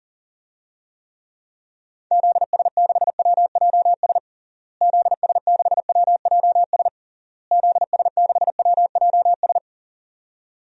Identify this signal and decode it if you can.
Morse code, heard as a keyed tone, about 40 wpm, 700 Hz. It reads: ZS6WJS ZS6WJS ZS6WJS
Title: DXCC-100 Call Signs - 6 Characters - Repeated 3x - 40wpm